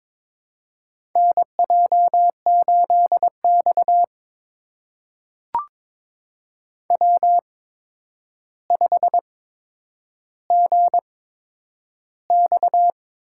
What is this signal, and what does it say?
Morse code, heard as a keyed tone, about 22 wpm, spaced as for 9 wpm, 700 Hz. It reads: NJ8X W5GX
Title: USA Call Signs - 4 Characters - 8x Character Spacing - 22wpm